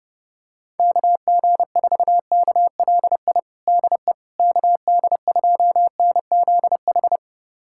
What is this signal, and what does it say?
Morse code, heard as a keyed tone, about 32 wpm, 700 Hz. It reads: KG4KLI DE KD2NZH
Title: USA Callsign DE Callsign 30wpm